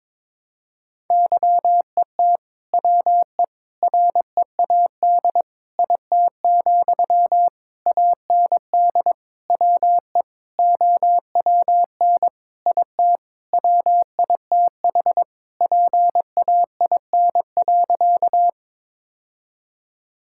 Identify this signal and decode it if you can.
Morse code, heard as a keyed tone, about 21 wpm, 700 Hz. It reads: YET WE READ IT, AND WE OWN IT WITH PAIN.
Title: Sentences from Top 1600 Words 22wpm